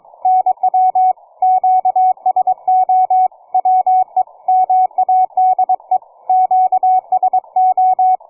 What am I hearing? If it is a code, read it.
Morse code, heard as a keyed tone, about 22 wpm, 740 Hz, with radio noise. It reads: NW QSO WE MADE QSO
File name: cw forum / 9m2ax 160m1829 22h14